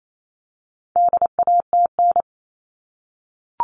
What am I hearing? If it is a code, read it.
Morse code, heard as a keyed tone, about 28 wpm, 700 Hz. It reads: DATN
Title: All Cut Numbers - 4-Digit 28wpm